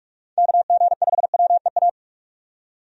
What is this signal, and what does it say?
Morse code, heard as a keyed tone, about 42 wpm, 700 Hz. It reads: KG5WEA
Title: USA Call Signs - 6 Characters - 45wpm